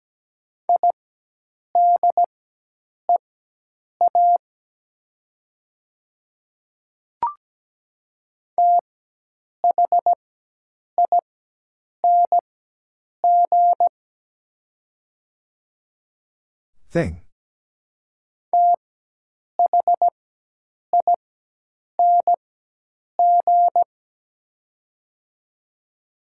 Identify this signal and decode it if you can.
Morse code, heard as a keyed tone, about 17 wpm, spaced as for 8 wpm, 700 Hz. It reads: IDEA THING THING
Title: 4x Character Spacing - Top 300 Words - 17wpm